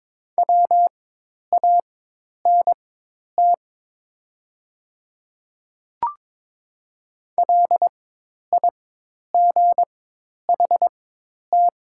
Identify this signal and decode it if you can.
Morse code, heard as a keyed tone, about 22 wpm, spaced as for 10 wpm, 700 Hz. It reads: WANT LIGHT
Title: 4x Character Spacing - Top 200 Words - 22wpm